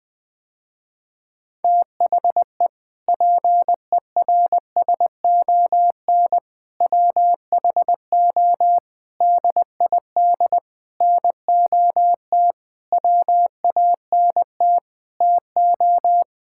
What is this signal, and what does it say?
Morse code, heard as a keyed tone, about 20 wpm, 700 Hz. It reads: THE PERSON WHO DID NOT WANT TO
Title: Sentences with 7-Word Phrases - Spoken Beginning - 20wpm